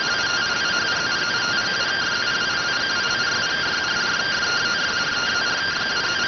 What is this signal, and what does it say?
Two-tone teleprinter signal, about 170 Hz shift.